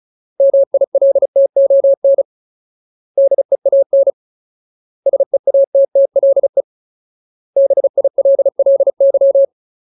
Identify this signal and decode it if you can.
Morse code, heard as a keyed tone, about 34 wpm, 550 Hz. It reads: MILTON DEAN SEATTLE BILLY
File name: Ultimate Word List / 35WPM Ultimate